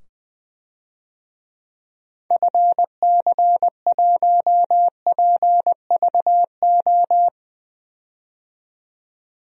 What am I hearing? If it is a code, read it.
Morse code, heard as a keyed tone, about 21 wpm, 700 Hz. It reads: FC1PVO
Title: DXCC-100 Call Signs - 6 Characters - 20wpm